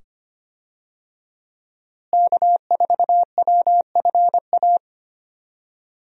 Morse code, heard as a keyed tone, about 25 wpm, 700 Hz. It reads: K4WFA
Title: USA Call Signs - 5 Characters - Repeated 2x - 25wpm